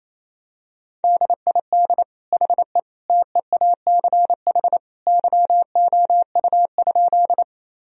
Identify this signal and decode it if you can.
Morse code, heard as a keyed tone, about 28 wpm, 700 Hz. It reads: DID HE TEACH YOU?
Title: Sentences from Top 1000 Words - Review Part 2 - 28wpm